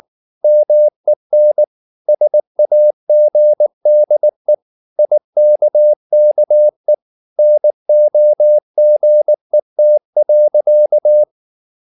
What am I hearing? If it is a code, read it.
Morse code, heard as a keyed tone, about 19 wpm, 600 Hz. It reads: MEN SAGDE IKKE NOGET.
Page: Ferie 19 wpm